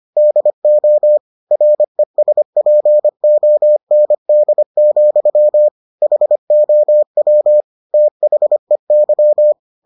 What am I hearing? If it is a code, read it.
Morse code, heard as a keyed tone, about 25 wpm, 600 Hz. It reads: DO RESPOND, HOW THEY